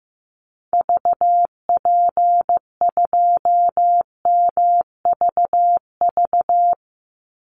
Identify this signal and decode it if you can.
Morse code, heard as a keyed tone, about 15 wpm, 700 Hz. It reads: VP2MVV
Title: Call Signs Repeated 3x - 15wpm